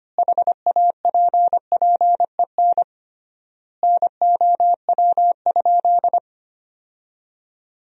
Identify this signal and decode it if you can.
Morse code, heard as a keyed tone, about 25 wpm, 700 Hz. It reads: HAPPEN NOW?
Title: 3x Word-Spacing - Sentences from Top 400 Words - 25wpm